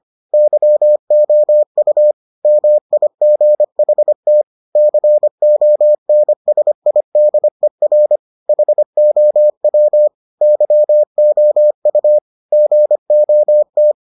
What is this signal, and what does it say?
Morse code, heard as a keyed tone, about 25 wpm, 600 Hz. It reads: YOU MIGHT CONSIDER HOW YOU GOT